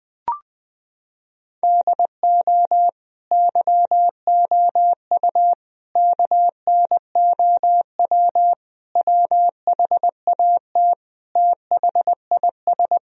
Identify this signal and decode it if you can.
Morse code, heard as a keyed tone, about 20 wpm, 700 Hz. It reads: DO YOU KNOW WHAT THIS
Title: Sentences from Top 1000 Words - Review Part 1 - 20wpm